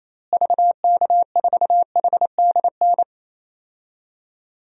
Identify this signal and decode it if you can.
Morse code, heard as a keyed tone, about 28 wpm, 700 Hz. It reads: VK4HDN
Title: DXCC-100 Call Signs - 6 Characters - Repeated 3x - 28wpm